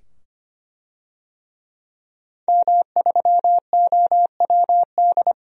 Morse code, heard as a keyed tone, about 25 wpm, 700 Hz. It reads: M3OWD
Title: DXCC-100 Call Signs - 5 Characters - 25wpm